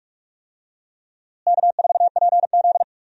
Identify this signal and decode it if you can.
Morse code, heard as a keyed tone, about 46 wpm, 700 Hz. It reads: K4PZ
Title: USA Call Signs - 4 Characters - Repeated 3x - 45wpm